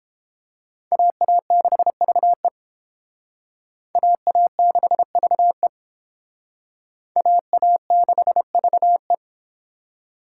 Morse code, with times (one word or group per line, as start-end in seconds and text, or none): 0.9-2.5 AA6VE
4.0-5.7 AA6VE
7.2-9.1 AA6VE